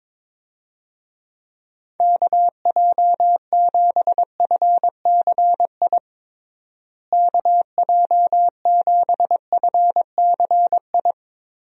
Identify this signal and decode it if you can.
Morse code, heard as a keyed tone, about 22 wpm, 700 Hz. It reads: KJ7FCI KJ7FCI
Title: USA Call Signs - 6 Characters - Repeated 3x - 22wpm